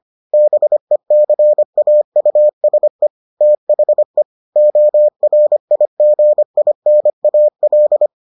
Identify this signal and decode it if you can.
Morse code, heard as a keyed tone, about 25 wpm, 600 Hz. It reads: BECAUSE THE ORIGINAL